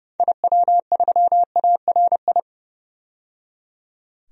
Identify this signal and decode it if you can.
Morse code, heard as a keyed tone, about 31 wpm, 700 Hz. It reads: IW3ARI
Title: DXCC-100 Call Signs - 6 Characters - 30wpm